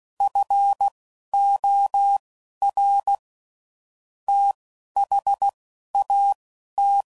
Morse code, heard as a keyed tone, about 16 wpm, spaced as for 11 wpm, 800 Hz. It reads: FOR THAT
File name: CW Code Groups @10WPM / QSOs / QSO WA7QOX